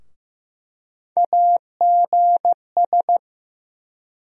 Morse code, heard as a keyed tone, about 15 wpm, 700 Hz. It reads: AGS
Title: USA Call Sign Postfixes - 3 Letters 15wpm